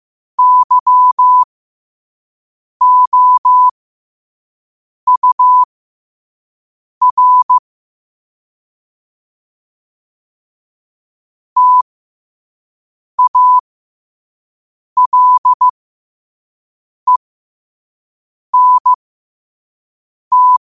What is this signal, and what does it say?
Morse code, heard as a keyed tone, about 15 wpm, spaced as for 5 wpm, 1000 Hz. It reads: YOUR TALENT
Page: New quotes every day in morse code at 5 Words per minute.